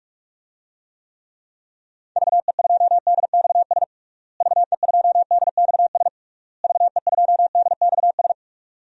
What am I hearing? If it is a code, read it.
Morse code, heard as a keyed tone, about 46 wpm, 700 Hz. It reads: VE2DXS VE2DXS VE2DXS
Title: DXCC-100 Call Signs - 6 Characters - Repeated 3x - 45wpm